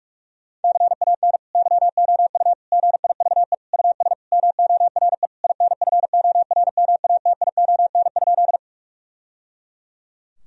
Morse code, heard as a keyed tone, about 41 wpm, 700 Hz. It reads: CAN YOU GIVE US MORE INFORMATION?
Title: Sentences from Top 1000 Words - Review Part 3 - 45wpm